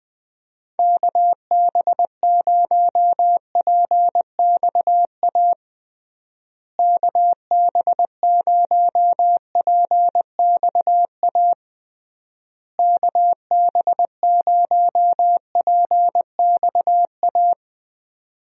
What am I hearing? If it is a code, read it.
Morse code, heard as a keyed tone, about 21 wpm, 700 Hz. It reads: KB0PXA KB0PXA KB0PXA
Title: USA Call Signs - 6 Characters - Repeated 3x - 20wpm